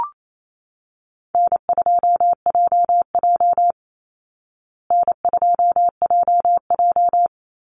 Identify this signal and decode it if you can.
Morse code, heard as a keyed tone, about 28 wpm, 700 Hz. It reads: N2JJ N2JJ
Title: Call Signs Repeated 3x - 28wpm